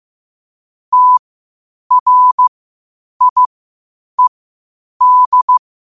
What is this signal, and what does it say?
Morse code, heard as a keyed tone, about 15 wpm, spaced as for 8 wpm, 1000 Hz. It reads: TRIED
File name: qotdfiles / 8wpm